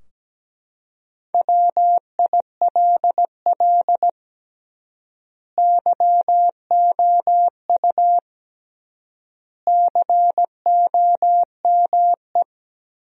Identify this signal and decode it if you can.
Morse code, heard as a keyed tone, about 17 wpm, 700 Hz. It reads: WILL YOU COME
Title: 3x Word-Spacing - Sentences from Top 200 Words - 17wpm